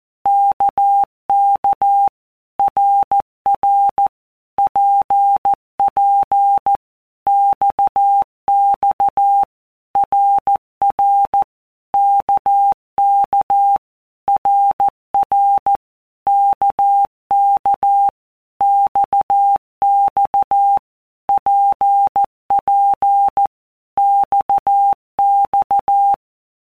Morse code, heard as a keyed tone, about 14 wpm, 800 Hz. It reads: KK RR PP XX RR KK RR KK XX PP XX
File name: All Groups 1-minute Blocks / Groups@13 / RKPX